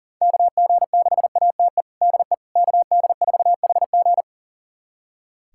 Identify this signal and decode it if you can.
Morse code, heard as a keyed tone, about 40 wpm, 700 Hz. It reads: KG6ATE DE KD4HG
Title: USA Callsign DE Callsign 40wpm